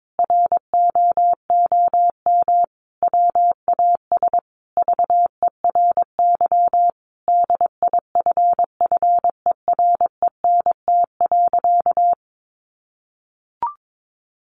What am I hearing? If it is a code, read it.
Morse code, heard as a keyed tone, about 22 wpm, 700 Hz. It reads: ROOM WAS VERY DIFFERENT.
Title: Sentences from Top 1300 Words 22wpm